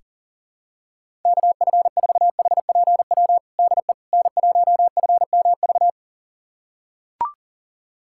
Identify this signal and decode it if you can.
Morse code, heard as a keyed tone, about 40 wpm, 700 Hz. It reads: KF4HPW DE N1FMV
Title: USA Callsign DE Callsign 40wpm